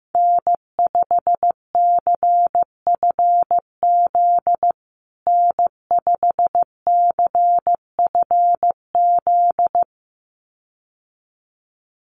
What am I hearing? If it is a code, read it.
Morse code, heard as a keyed tone, about 15 wpm, 700 Hz. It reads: N5CFZ N5CFZ
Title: USA Call Signs Repeated 2x 15wpm